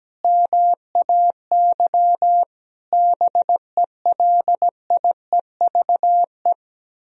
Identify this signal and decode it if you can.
Morse code, heard as a keyed tone, about 17 wpm, 700 Hz. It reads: MAY BELIEVE
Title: Sentences from Top 1000 Words - Review Part 4 - 17wpm